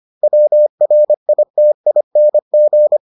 Morse code, heard as a keyed tone, about 25 wpm, 600 Hz. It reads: WRITING